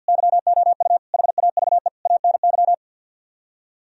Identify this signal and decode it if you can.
Morse code, heard as a keyed tone, about 64 wpm, 700 Hz. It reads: YOU HAVE ANY